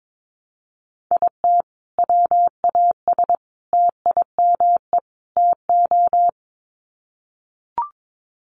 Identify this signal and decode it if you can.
Morse code, heard as a keyed tone, about 22 wpm, 700 Hz. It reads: IT WAS TIME TO
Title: Sets of 4 Words 22wpm